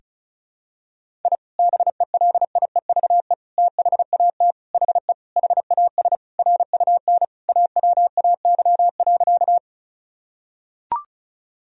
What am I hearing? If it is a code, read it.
Morse code, heard as a keyed tone, about 36 wpm, 700 Hz. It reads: I BELIEVE THAT HE HAS RUN AWAY.
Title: Sentences from Top 1000 Words - Review Part 1 - 35wpm